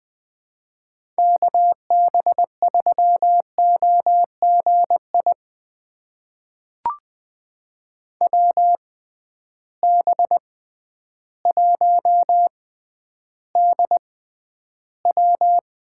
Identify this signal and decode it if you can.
Morse code, heard as a keyed tone, about 20 wpm, spaced as for 11 wpm, 700 Hz. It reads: KB3OGI WB1DW
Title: USA Call Signs - 6 Characters - 6x Character Spacing - 20wpm